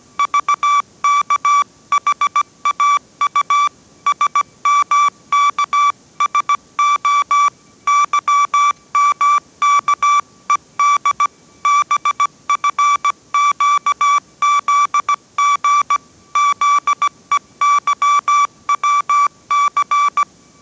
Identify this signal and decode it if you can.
Morse code, heard as a keyed tone, about 17 wpm, 1160 Hz. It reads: VKHAU SMKSO YMKED BFQZG ZEYWC